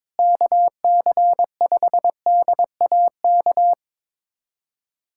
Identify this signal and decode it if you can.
Morse code, heard as a keyed tone, about 22 wpm, 700 Hz. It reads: KC5DAK
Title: USA Callsign DE Callsign 22wpm